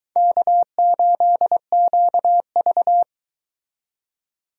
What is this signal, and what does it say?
Morse code, heard as a keyed tone, about 23 wpm, 700 Hz. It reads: K8QV